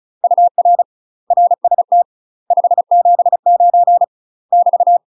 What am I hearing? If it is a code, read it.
Morse code, heard as a keyed tone, about 34 wpm, 700 Hz. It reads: UR RST 579 <BT>
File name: UK MASTER QSOs TRAINING / 35WPM / UK MASTER QSOs 0041